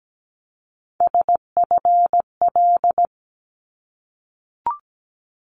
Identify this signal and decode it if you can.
Morse code, heard as a keyed tone, about 17 wpm, 700 Hz. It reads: SFL
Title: ARRL Sections 17wpm